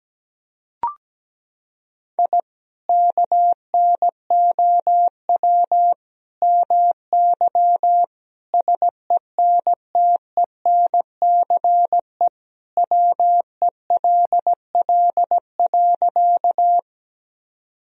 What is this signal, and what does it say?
Morse code, heard as a keyed tone, about 17 wpm, 700 Hz. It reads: I KNOW MY SENTENCE WELL.
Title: Sentences from Top 1000 Words - Review Part 1 - 17wpm